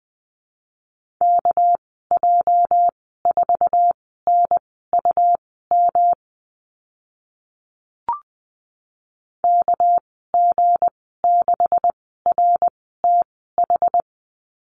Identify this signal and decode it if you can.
Morse code, heard as a keyed tone, about 20 wpm, spaced as for 14 wpm, 700 Hz. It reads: KJ4NUM KG6RTH
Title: 245 Lesson 39g - Letter z - Call Signs 20wpm